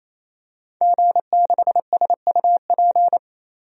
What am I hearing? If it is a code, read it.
Morse code, heard as a keyed tone, about 28 wpm, 700 Hz. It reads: G6SUP
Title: DXCC-100 Call Signs - 5 Characters - 28wpm